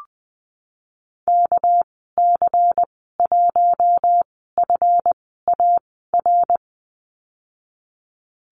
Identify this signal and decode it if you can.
Morse code, heard as a keyed tone, about 20 wpm, spaced as for 15 wpm, 700 Hz. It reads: KC1FAR
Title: 200 Lesson 32g - Letter k - Call Signs 20wpm